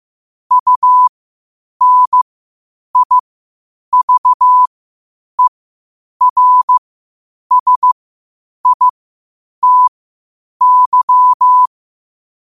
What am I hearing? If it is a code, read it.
Morse code, heard as a keyed tone, about 15 wpm, spaced as for 8 wpm, 1000 Hz. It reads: UNIVERSITY
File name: qotdfiles / 8wpm